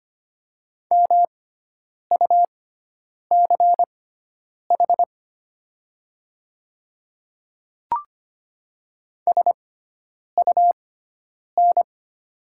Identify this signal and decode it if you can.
Morse code, heard as a keyed tone, about 25 wpm, spaced as for 9 wpm, 700 Hz. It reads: MUCH SUN
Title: 6x Character Spacing - Top 300 Words - 25wpm